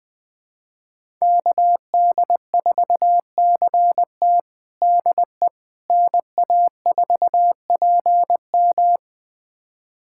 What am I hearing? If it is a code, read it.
Morse code, heard as a keyed tone, about 20 wpm, 700 Hz. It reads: KD4CT DE NA4PM
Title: USA Callsign DE Callsign - 5 Characters - 20wpm